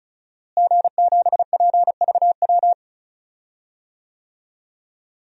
Morse code, heard as a keyed tone, about 36 wpm, 700 Hz. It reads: G7PVW
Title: DXCC-100 Call Signs - 5 Characters - Repeated 3x - 35wpm